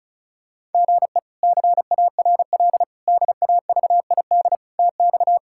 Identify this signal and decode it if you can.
Morse code, heard as a keyed tone, about 36 wpm, 700 Hz. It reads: GE CARL DAVID TX
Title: K1USN SST QSO Practice - 35wpm